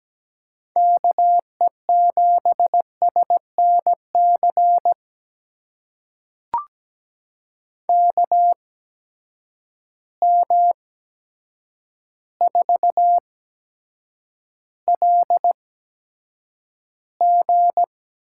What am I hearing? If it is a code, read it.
Morse code, heard as a keyed tone, about 17 wpm, spaced as for 8 wpm, 700 Hz. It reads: KE7SNC KM4LG
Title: USA Call Signs - 6 Characters - 8x Character Spacing - 17wpm